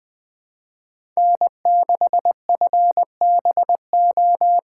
Morse code, heard as a keyed tone, about 20 wpm, 700 Hz. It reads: N6FBO
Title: USA Call Signs - 4x Character Spacing - 20wpm